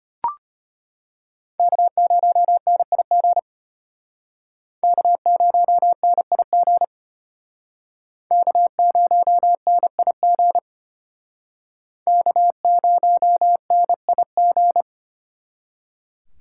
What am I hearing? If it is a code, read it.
Morse code, heard as a keyed tone, about 32 wpm, 700 Hz. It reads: K0NIG K0NIG K0NIG K0NIG